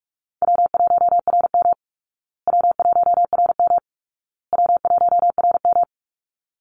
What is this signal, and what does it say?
Morse code, heard as a keyed tone, about 44 wpm, 700 Hz. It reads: W1RM W1RM W1RM
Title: Easy Call Signs Repeated 3x - 45wpm